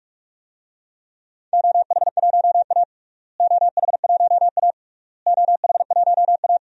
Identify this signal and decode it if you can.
Morse code, heard as a keyed tone, about 41 wpm, 700 Hz. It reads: OH1A OH1A OH1A
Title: DXCC-100 Call Signs - 4 Characters - Repeated 3x - 45wpm